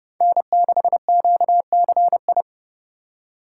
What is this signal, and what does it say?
Morse code, heard as a keyed tone, about 30 wpm, 700 Hz. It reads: N6QCI
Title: USA Call Signs - 4x Character Spacing - 30wpm